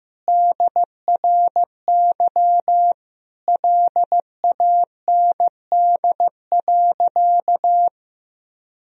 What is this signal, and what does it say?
Morse code, heard as a keyed tone, about 15 wpm, 700 Hz. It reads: DRY LAND.
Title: Sentences from Top 1000 Words - Review Part 3 - 15wpm